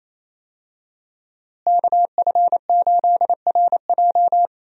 Morse code, heard as a keyed tone, about 27 wpm, 700 Hz. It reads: KF8RJ
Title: USA Call Signs - 5 Characters - Repeated 3x - 28wpm